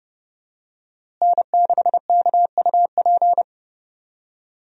Morse code, heard as a keyed tone, about 30 wpm, 700 Hz. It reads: N6KUP
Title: USA Call Signs - 5 Characters - Repeated 3x - 30wpm